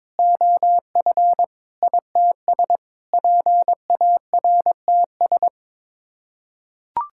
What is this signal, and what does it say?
Morse code, heard as a keyed tone, about 22 wpm, 700 Hz. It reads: OF ITS PARTS
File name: practice / 22wpm / Sets of 8 Words - 22wpm